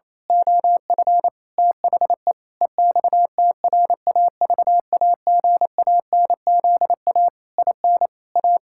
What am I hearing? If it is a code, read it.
Morse code, heard as a keyed tone, about 27 wpm, 700 Hz. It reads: OF THE EXTRAVAGANZA IN A
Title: Pangram Challenge v2 - 28wpm